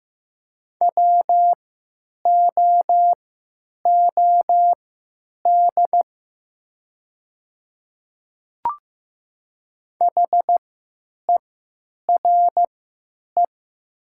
Morse code, heard as a keyed tone, about 15 wpm, spaced as for 9 wpm, 700 Hz. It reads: WOOD HERE
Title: 3x Character Spacing - Top 500 Words - 15wpm